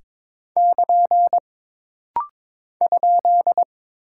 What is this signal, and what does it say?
Morse code, heard as a keyed tone, about 22 wpm, 700 Hz. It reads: <KN> ?
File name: practice / 22wpm / Single Punctuation-Prosign - Rapid-Fire - 22wpm